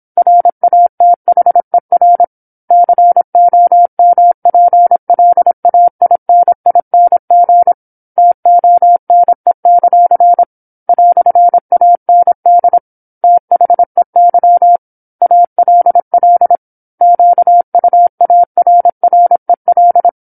Morse code, heard as a keyed tone, about 27 wpm, 700 Hz. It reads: RATHER COMPLAINING TONE; "AND THEY ALL QUARREL
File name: 08-Chapter-08 26 WPM